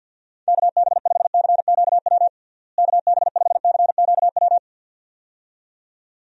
Morse code, heard as a keyed tone, about 52 wpm, 700 Hz. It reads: KB5CQW KB5CQW
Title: USA Call Signs - 6 Characters - Repeated 2x - 50wpm